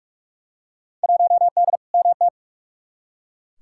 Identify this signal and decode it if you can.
Morse code, heard as a keyed tone, about 43 wpm, 700 Hz. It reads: 1D MT
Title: Field Day Exchanges - 45wpm